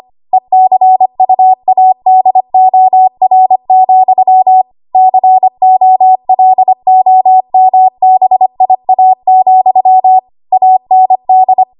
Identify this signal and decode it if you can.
Morse code, heard as a keyed tone, about 25 wpm, 750 Hz. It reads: ECUADOR, COLOMBIA, AND
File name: Morse / 25WPM